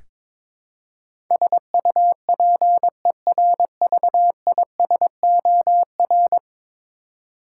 Morse code, heard as a keyed tone, about 21 wpm, 700 Hz. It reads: SUPERVISOR
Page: morse-code-ninja-practice-22wpm